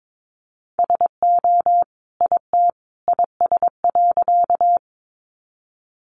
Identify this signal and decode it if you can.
Morse code, heard as a keyed tone, about 22 wpm, 700 Hz. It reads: SO IT IS.
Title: Sentences from Top 100 Words 22wpm